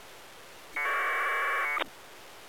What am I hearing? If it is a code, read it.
Packet radio (AX.25, as in APRS).